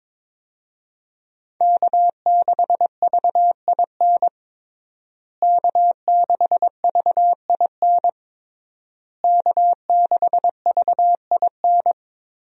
Morse code, heard as a keyed tone, about 21 wpm, 700 Hz. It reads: K6VIN K6VIN K6VIN
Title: USA Call Signs - 5 Characters - Repeated 3x - 22wpm